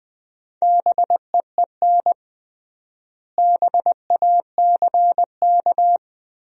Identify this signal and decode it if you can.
Morse code, heard as a keyed tone, about 20 wpm, 700 Hz. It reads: BEEN BACK
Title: 3x Word-Spacing - Sentences from Top 200 Words - 20wpm